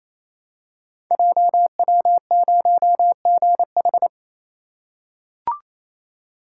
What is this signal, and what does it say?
Morse code, heard as a keyed tone, about 28 wpm, 700 Hz. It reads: JW0GH
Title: Code Groups - 5 Letter:Numbers - 28wpm